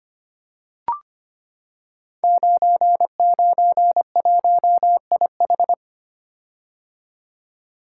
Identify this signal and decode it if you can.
Morse code, heard as a keyed tone, about 25 wpm, 700 Hz. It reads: 991IH
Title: Code Groups - 5 Letter:Numbers - 25wpm